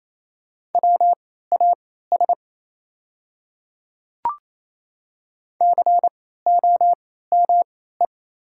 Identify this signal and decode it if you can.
Morse code, heard as a keyed tone, about 28 wpm, spaced as for 16 wpm, 690 Hz. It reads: WAS COME
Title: 3x Character Spacing - Top 100 Words - 28wpm